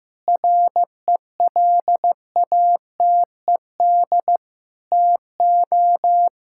Morse code, heard as a keyed tone, about 15 wpm, 700 Hz. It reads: RELATED TO
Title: Sentences from Top 1000 Words - Review Part 5 - 15wpm